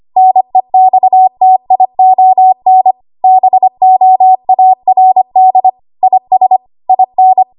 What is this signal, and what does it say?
Morse code, heard as a keyed tone, about 25 wpm, 750 Hz. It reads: NEXTION BOARD IS IN
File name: Archive / 25 WPM / 25WPM